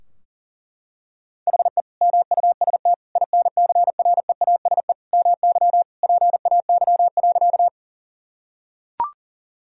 Morse code, heard as a keyed tone, about 40 wpm, 700 Hz. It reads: HE MUST INCREASE MY PAY.
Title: Sentences from Top 1000 Words - Review Part 4 - 40wpm